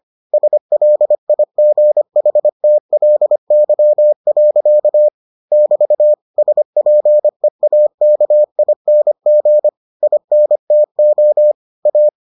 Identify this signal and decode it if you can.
Morse code, heard as a keyed tone, about 25 wpm, 600 Hz. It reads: SLIGHTLY. <BT> SPEAKING INTO A